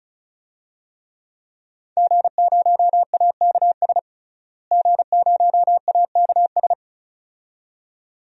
Morse code, heard as a keyed tone, about 36 wpm, 700 Hz. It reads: G0AKS G0AKS
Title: DXCC-100 Call Signs - 5 Characters - Repeated 2x - 35wpm